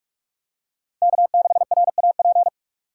Morse code, heard as a keyed tone, about 46 wpm, 700 Hz. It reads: K6RAP
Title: USA Call Signs - 6x Character Spacing - 45wpm